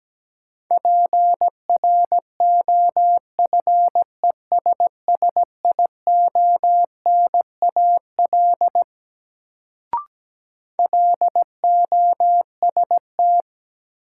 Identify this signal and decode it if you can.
Morse code, heard as a keyed tone, about 17 wpm, 700 Hz. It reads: PROFESSIONAL LOST
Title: Sentences from Top 1400 Words - New Word Drill Alt - 17wpm